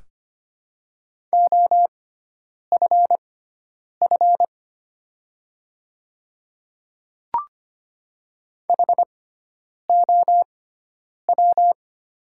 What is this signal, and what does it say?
Morse code, heard as a keyed tone, about 25 wpm, spaced as for 9 wpm, 700 Hz. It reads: OFF HOW
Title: 6x Character Spacing - Top 300 Words - 25wpm